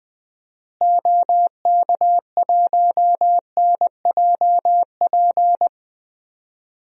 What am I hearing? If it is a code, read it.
Morse code, heard as a keyed tone, about 20 wpm, 700 Hz. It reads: OK1NJP
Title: DXCC-100 Call Signs - 6 Characters - Repeated 3x - 20wpm